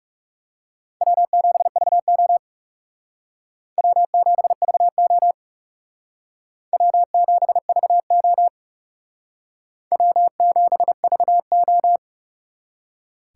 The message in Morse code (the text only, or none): W7VO W7VO W7VO W7VO